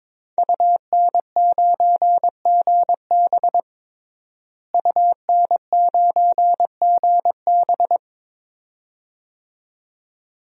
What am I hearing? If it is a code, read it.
Morse code, heard as a keyed tone, about 22 wpm, 700 Hz. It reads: UN9GB UN9GB
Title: DXCC-100 Call Signs - 5 Characters - Repeated 2x - 22wpm